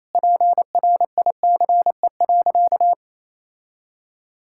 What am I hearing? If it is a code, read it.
Morse code, heard as a keyed tone, about 28 wpm, 700 Hz. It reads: PRICE.
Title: Sentences from Top 1000 Words - Review Part 6 - 28wpm